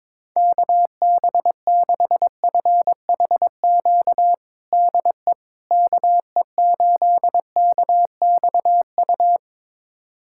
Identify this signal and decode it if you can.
Morse code, heard as a keyed tone, about 22 wpm, 700 Hz. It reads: KB6FHQ DE KE8KXU
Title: USA Callsign DE Callsign 22wpm